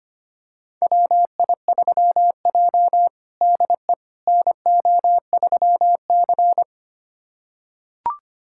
Morse code, heard as a keyed tone, about 25 wpm, 700 Hz. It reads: WI3J DE NO3C
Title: USA Callsign DE Callsign - 4 Characters - 25wpm